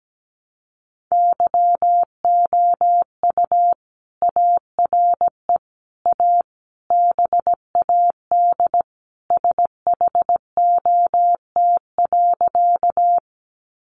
Morse code, heard as a keyed tone, about 17 wpm, 700 Hz. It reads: YOU ARE A BAD SHOT.
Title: Sentences from Top 1100 Words 17wpm